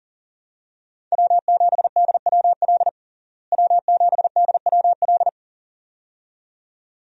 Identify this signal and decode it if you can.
Morse code, heard as a keyed tone, about 40 wpm, 700 Hz. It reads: W7DWL W7DWL
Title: USA Call Signs - 5 Characters - Repeated 2x - 40wpm